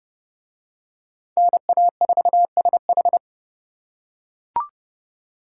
Morse code, heard as a keyed tone, about 26 wpm, 700 Hz. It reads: NA4SH